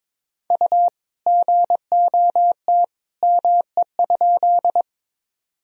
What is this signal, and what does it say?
Morse code, heard as a keyed tone, about 21 wpm, 700 Hz. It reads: U GOT ME?
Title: Sentences from Top 300 Words and Abbreviations 22wpm